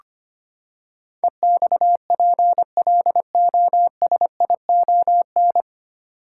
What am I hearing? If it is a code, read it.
Morse code, heard as a keyed tone, about 25 wpm, 700 Hz. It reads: EXPLOSION
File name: Nine Letter Words - 25wpm